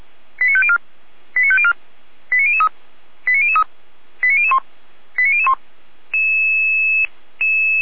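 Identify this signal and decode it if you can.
Five-tone selective call (ZVEI), heard as quick runs of five tones, tones of about 70 ms. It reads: 89504 89504 89003 89003 89001 89001
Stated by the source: ZVEI five-tone selective call received on SDR.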